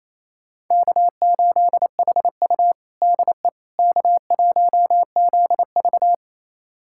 Morse code, heard as a keyed tone, about 27 wpm, 700 Hz. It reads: K8HU DE K1ZV
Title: USA Callsign DE Callsign - 4 Characters - 28wpm